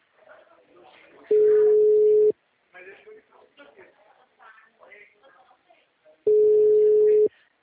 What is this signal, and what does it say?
Ringback tone (Europe, 425 Hz), with speech.